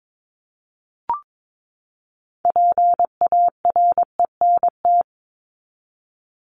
Morse code, heard as a keyed tone, about 22 wpm, 700 Hz. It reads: PARENT
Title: ES:S Suffix - Top 100 Words with Base Words 22wpm